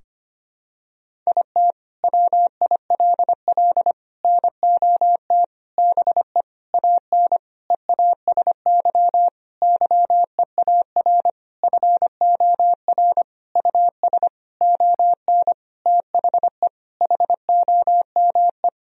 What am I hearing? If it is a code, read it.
Morse code, heard as a keyed tone, about 25 wpm, 700 Hz. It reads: IT WILL NOT BE AN EASY YEAR FOR US ON THE HOME
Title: Sentences from Top 1000 Words - Review Part 2 - 25wpm